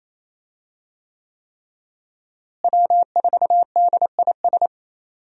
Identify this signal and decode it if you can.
Morse code, heard as a keyed tone, about 27 wpm, 700 Hz. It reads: W4DIS
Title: USA Call Signs - 5 Characters - Repeated 3x - 28wpm